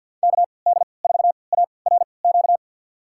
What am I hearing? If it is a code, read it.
Morse code, heard as a keyed tone, about 48 wpm, spaced as for 28 wpm, 690 Hz. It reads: KD4ARQ
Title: USA Call Signs - 6 Characters - 3x Character Spacing - 50wpm